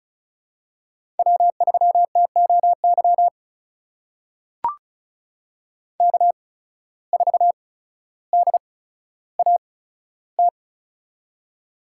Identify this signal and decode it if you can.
Morse code, heard as a keyed tone, about 36 wpm, spaced as for 15 wpm, 700 Hz. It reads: W3TOY K4DAT
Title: USA Call Signs - 5 Characters - 8x Character Spacing - 35wpm